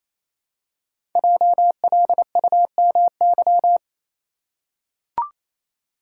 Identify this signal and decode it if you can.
Morse code, heard as a keyed tone, about 29 wpm, 700 Hz. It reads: JLUMY
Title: Code Groups - 5 Letters - 28wpm